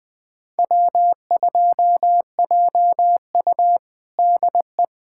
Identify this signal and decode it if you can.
Morse code, heard as a keyed tone, about 20 wpm, 700 Hz. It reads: W2JU DE
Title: USA Callsign DE Callsign - 4 Characters - 20wpm